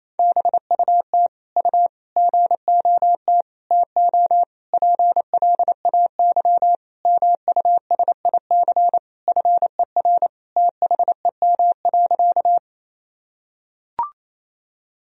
Morse code, heard as a keyed tone, about 28 wpm, 700 Hz. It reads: BUT U GOT TO PLAY MUSIC FER THEM.
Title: Sentences from Top 300 Words and Abbreviations 28wpm